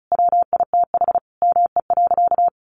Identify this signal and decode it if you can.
Morse code, heard as a keyed tone, about 34 wpm, 700 Hz. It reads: WITH ME.
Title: Sentences from Top 1300 Words 35wpm